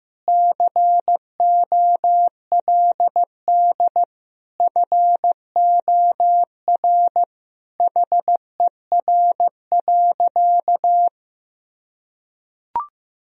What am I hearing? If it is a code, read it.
Morse code, heard as a keyed tone, about 15 wpm, 700 Hz. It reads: COLD FOR HER.
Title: Sentences from Top 1000 Words - Review Part 4 - 15wpm